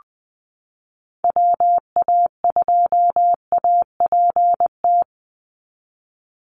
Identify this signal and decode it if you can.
Morse code, heard as a keyed tone, about 20 wpm, 700 Hz. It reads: WA2APT
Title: USA Call Signs 20wpm